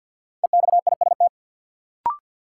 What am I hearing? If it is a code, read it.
Morse code, heard as a keyed tone, about 53 wpm, 700 Hz. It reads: EXIST
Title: Sentences from Top 1100 Words - New Word Drill Alt - 50wpm